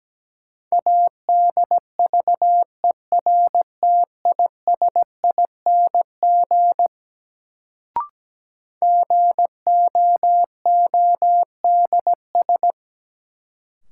Morse code, heard as a keyed tone, about 17 wpm, 700 Hz. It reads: ADVERTISING GOODS
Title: Sentences from Top 1000 Words - Review Part 6 - New Word Drill - 17wpm